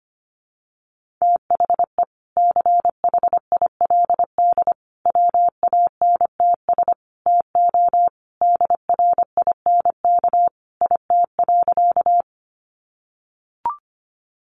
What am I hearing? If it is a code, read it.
Morse code, heard as a keyed tone, about 25 wpm, 700 Hz. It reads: THE CHILD WANTS TO DRINK IT.
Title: Additional Easy Phrases 25wpm